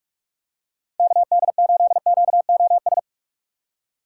Morse code, heard as a keyed tone, about 45 wpm, 700 Hz. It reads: KD8QOS
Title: USA Call Signs - 6 Characters - 4x Character Spacing - 45wpm